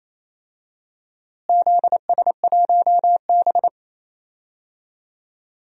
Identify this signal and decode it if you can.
Morse code, heard as a keyed tone, about 28 wpm, 700 Hz. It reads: ZS1B
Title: DXCC-100 Call Signs - 4 Characters - Repeated 2x - 28wpm